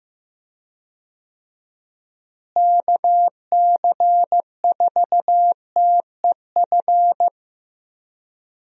Morse code, heard as a keyed tone, about 16 wpm, 700 Hz. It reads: KC4TEF
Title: USA Call Signs - 6 Characters - Repeated 2x - 15wpm